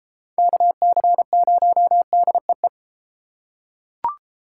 Morse code, heard as a keyed tone, about 29 wpm, 700 Hz. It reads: KC0DEE